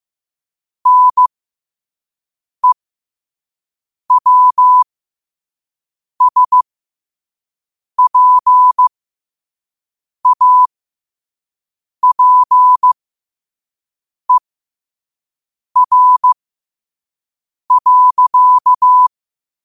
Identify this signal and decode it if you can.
Morse code, heard as a keyed tone, about 15 wpm, spaced as for 5 wpm, 1000 Hz. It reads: NEWSPAPER.
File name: qotdfiles / 5wpm